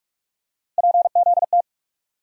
Morse code, heard as a keyed tone, about 46 wpm, 700 Hz. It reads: PZT
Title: USA Call Sign Postfixes - 3 Letters 45wpm